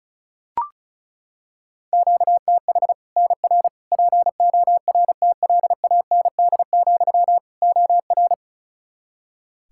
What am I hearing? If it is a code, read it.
Morse code, heard as a keyed tone, about 34 wpm, 700 Hz. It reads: QTH NR PORTLAND, OR
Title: Practice QSO - Exchange 1 - Beginning - 35wpm